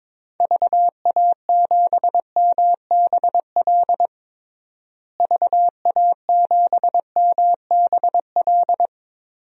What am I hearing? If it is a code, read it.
Morse code, heard as a keyed tone, about 22 wpm, 700 Hz. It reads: VA7MBL VA7MBL
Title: DXCC-100 Call Signs - 6 Characters - Repeated 2x - 22wpm